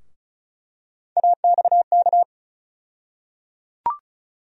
Morse code, heard as a keyed tone, about 36 wpm, 700 Hz. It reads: AXK
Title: USA Call Sign Postfixes - 3 Letters 35wpm